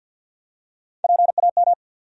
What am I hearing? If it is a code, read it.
Morse code, heard as a keyed tone, about 52 wpm, 700 Hz. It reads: PAM
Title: USA Common Names - Max 4 Letters - 50wpm